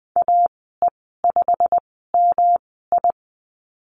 Morse code, heard as a keyed tone, about 20 wpm, spaced as for 14 wpm, 700 Hz. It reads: AE5MI
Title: 186 Lesson 30g - Letter b - Call Signs 20wpm